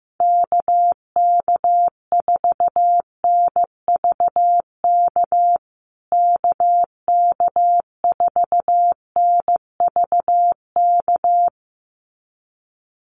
Morse code, heard as a keyed tone, about 15 wpm, 700 Hz. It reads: KK4NVK KK4NVK
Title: USA Call Signs Repeated 2x 15wpm